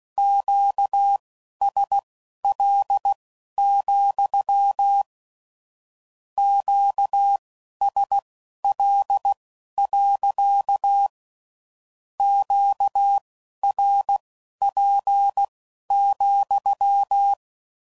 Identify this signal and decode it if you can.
Morse code, heard as a keyed tone, about 16 wpm, spaced as for 11 wpm, 800 Hz. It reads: QSL, QSL. QRP,
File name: CW Code Groups @15WPM / Q Calls / Q Calls sppc